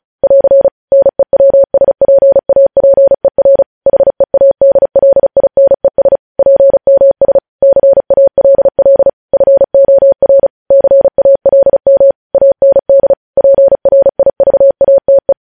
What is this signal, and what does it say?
Morse code, heard as a keyed tone, about 36 wpm, 550 Hz. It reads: <AR> NEWSPAPER HEADLINES PMS CALL FOR CALM AND PRIVATE